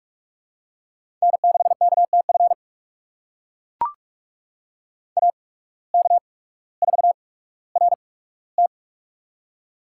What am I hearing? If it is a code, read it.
Morse code, heard as a keyed tone, about 43 wpm, spaced as for 19 wpm, 700 Hz. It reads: N6KTF AK4RT
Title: USA Call Signs - 5 Characters - 8x Character Spacing - 45wpm